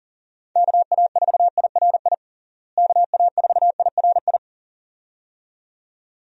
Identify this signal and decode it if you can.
Morse code, heard as a keyed tone, about 37 wpm, 700 Hz. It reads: KA4IRI KA4IRI
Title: USA Call Signs - 6 Characters - Repeated 2x - 40wpm